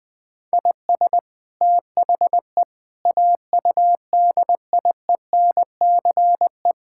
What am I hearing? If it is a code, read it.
Morse code, heard as a keyed tone, about 20 wpm, 700 Hz. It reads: IS THE AUDIENCE
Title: Sentences from Top 1000 Words - Review Part 5 - 20wpm